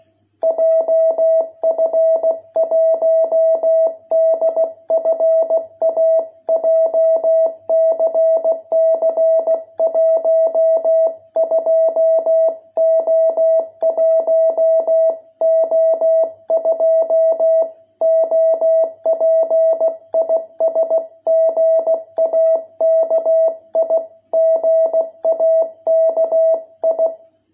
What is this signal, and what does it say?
Morse code, heard as a keyed tone, about 16 wpm, 660 Hz. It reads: JF1DFA JCC 12O1O2 OP IS GAKI GAKI